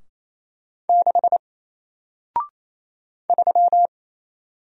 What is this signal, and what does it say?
Morse code, heard as a keyed tone, about 27 wpm, 700 Hz. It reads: 6 3
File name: Single-Letter-Number-28wpm